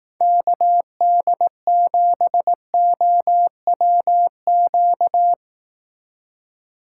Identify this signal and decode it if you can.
Morse code, heard as a keyed tone, about 18 wpm, 700 Hz. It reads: KD7OWQ
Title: Speed Racing - USA 2x3 Call Signs 15wpm